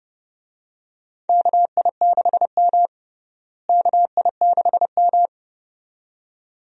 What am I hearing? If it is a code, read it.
Morse code, heard as a keyed tone, about 32 wpm, 700 Hz. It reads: KI6M KI6M
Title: USA Call Signs - 4 Characters - Repeated 2x - 30wpm